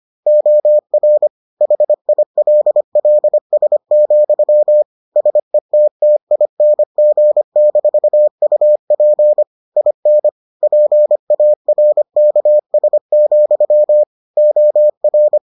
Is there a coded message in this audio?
Yes: Morse code from 0.3 to 15.4 s, heard as a keyed tone, about 25 wpm, 600 Hz. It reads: OR HILLS, SETTING-UP IN PARKS, OR